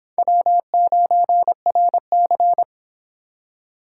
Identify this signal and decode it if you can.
Morse code, heard as a keyed tone, about 26 wpm, 700 Hz. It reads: W9RC